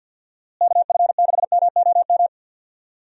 Morse code, heard as a keyed tone, about 52 wpm, 700 Hz. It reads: KF6MOM
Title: USA Call Signs - 6 Characters - Repeated 3x - 50wpm